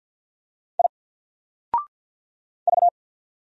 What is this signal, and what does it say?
Morse code, heard as a keyed tone, about 49 wpm, 700 Hz. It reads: I V
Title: Single Character (Letter:Number:Punctuation:Prosign) 50wpm